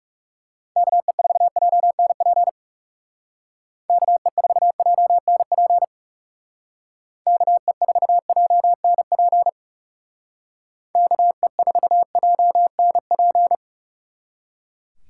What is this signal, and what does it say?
Morse code, heard as a keyed tone, about 36 wpm, 700 Hz. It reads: KE4JNP KE4JNP KE4JNP KE4JNP
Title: Speed Racing - USA 2x3 Call Signs 30wpm